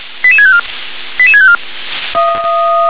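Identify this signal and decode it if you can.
Five-tone selective call (ZVEI), heard as quick runs of five tones, tones of about 70 ms. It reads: 88654 88654
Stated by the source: ZVEI five-tone call received on an SDR.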